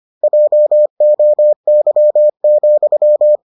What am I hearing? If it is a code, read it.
Morse code, heard as a keyed tone, about 25 wpm, 600 Hz. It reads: JOY,